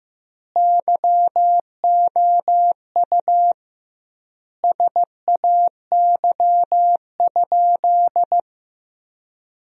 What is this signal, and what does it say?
Morse code, heard as a keyed tone, about 16 wpm, 700 Hz. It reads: YOU SAY?
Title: 2x Word-Spacing - Sentences from Top 200 Words - 15wpm